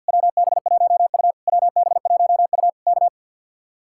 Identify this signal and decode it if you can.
Morse code, heard as a keyed tone, about 65 wpm, spaced as for 57 wpm, 700 Hz. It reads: WB1U WB1U K
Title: Speed-Racing Common QSO Phrases - 50wpm